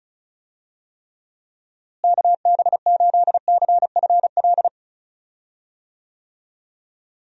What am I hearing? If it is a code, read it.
Morse code, heard as a keyed tone, about 34 wpm, 700 Hz. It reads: KB8CFL
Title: USA Call Signs - 6 Characters - Repeated 2x - 35wpm